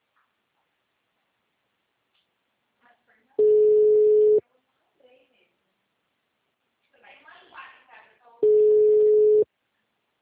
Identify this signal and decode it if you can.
Ringback tone (Europe, 425 Hz), with speech.